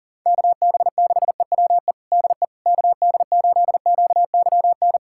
Morse code, heard as a keyed tone, about 40 wpm, 700 Hz. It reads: KB6EWE DE KD8QYN
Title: USA Callsign DE Callsign - 6 Characters - 40wpm